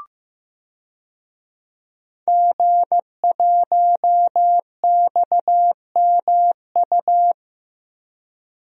Morse code, heard as a keyed tone, about 15 wpm, 700 Hz. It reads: G1XMU
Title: DXCC-100 Call Signs - 5 Characters - 15wpm